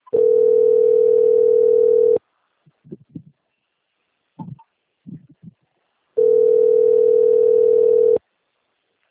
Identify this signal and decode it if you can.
Ringback tone (North America), 440 + 480 Hz.
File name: out- - - - - .2642